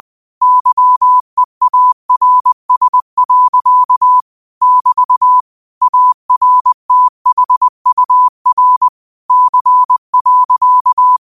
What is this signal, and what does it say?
Morse code, heard as a keyed tone, about 20 wpm, 1000 Hz. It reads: YEARS. <BT> ARTHUR C.
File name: qotdfiles / 20wpm